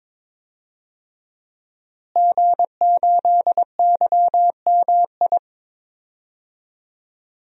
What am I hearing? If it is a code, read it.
Morse code, heard as a keyed tone, about 22 wpm, 700 Hz. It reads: G8YMI
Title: DXCC-100 Call Signs - 5 Characters - Repeated 3x - 22wpm